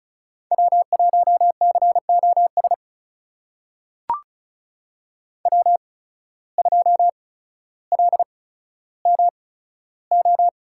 Morse code, heard as a keyed tone, about 36 wpm, spaced as for 15 wpm, 700 Hz. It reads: W1COS W2LMO
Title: USA Call Signs - 5 Characters - 8x Character Spacing - 35wpm